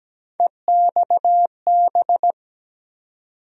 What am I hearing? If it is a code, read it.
Morse code, heard as a keyed tone, about 17 wpm, 700 Hz. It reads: EXB
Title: USA Call Sign Postfixes - 3 Letters 17wpm